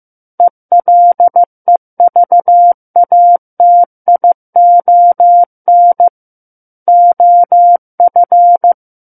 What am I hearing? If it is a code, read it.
Morse code, heard as a keyed tone, about 15 wpm, 700 Hz. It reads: ELEVATION OF
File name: Book One / 06-Chapter 6 - 15 WPM